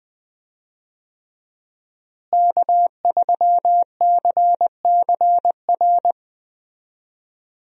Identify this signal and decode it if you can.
Morse code, heard as a keyed tone, about 20 wpm, 700 Hz. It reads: K3CCR
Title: USA Call Signs - 5 Characters - Repeated 3x - 20wpm